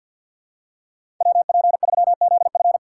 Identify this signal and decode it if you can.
Morse code, heard as a keyed tone, about 50 wpm, 700 Hz. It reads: WP3ZF
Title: USA Call Signs - 5 Characters - 3x Character Spacing - 50wpm